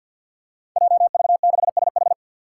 Morse code, heard as a keyed tone, about 48 wpm, 700 Hz. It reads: JV6SH
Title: Five Challenging Letter-Number Combinations 50wpm